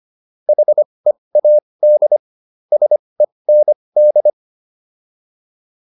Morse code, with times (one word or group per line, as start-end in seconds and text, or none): 0.5-2.2 HEAD
2.7-4.3 SEND